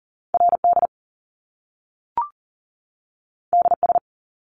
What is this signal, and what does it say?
Morse code, heard as a keyed tone, about 41 wpm, 700 Hz. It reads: RD DS
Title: Top Two Letter Combinations in Words 40wpm